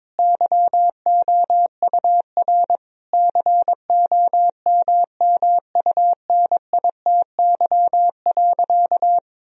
Morse code, heard as a keyed tone, about 22 wpm, 700 Hz. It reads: YOUR COMMUNITY.
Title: Sentences from Top 1000 Words - Review Part 4 - 22wpm